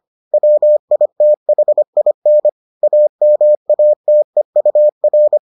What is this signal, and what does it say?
Morse code, heard as a keyed tone, about 25 wpm, 600 Hz. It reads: WITHIN AMATEUR